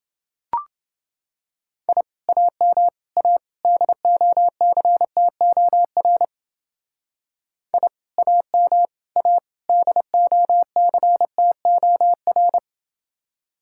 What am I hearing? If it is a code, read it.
Morse code, heard as a keyed tone, about 29 wpm, 700 Hz. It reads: I AM A DOCTOR I AM A DOCTOR